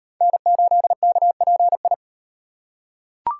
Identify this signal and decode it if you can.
Morse code, heard as a keyed tone, about 34 wpm, 700 Hz. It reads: N8KPI